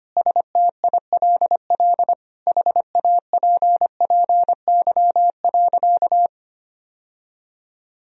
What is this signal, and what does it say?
Morse code, heard as a keyed tone, about 25 wpm, 700 Hz. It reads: STILL HAPPY.
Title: Sentences from Top 1000 Words - Review Part 3 - 25wpm